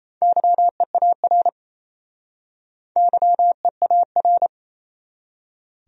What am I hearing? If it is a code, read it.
Morse code, heard as a keyed tone, about 36 wpm, 700 Hz. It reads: YEAR YEAR